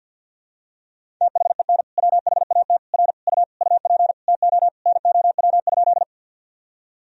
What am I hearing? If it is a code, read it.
Morse code, heard as a keyed tone, about 47 wpm, 700 Hz. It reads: THEN WHAT R U UP TO NOW?
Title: Sentences from Top 100 Words and Abbreviations 50wpm